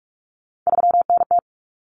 Morse code, heard as a keyed tone, about 44 wpm, 700 Hz. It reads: 3NT
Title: Common Cut Numbers - 3-Digit 45wpm